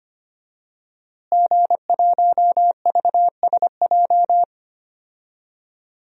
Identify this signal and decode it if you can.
Morse code, heard as a keyed tone, about 25 wpm, 700 Hz. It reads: G1VSJ
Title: DXCC-100 Call Signs - 5 Characters - Repeated 2x - 25wpm